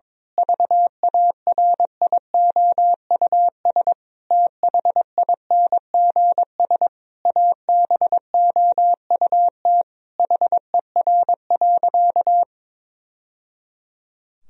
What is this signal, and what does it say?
Morse code, heard as a keyed tone, about 21 wpm, 700 Hz. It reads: VARIOUS THINGS ABOUT HER.
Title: Sentences from Top 1000 Words - Review Part 4 - 22wpm